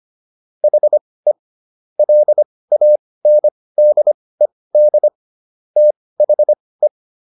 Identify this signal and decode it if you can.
Morse code, heard as a keyed tone, about 25 wpm, spaced as for 18 wpm, 600 Hz. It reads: HE LANDED THE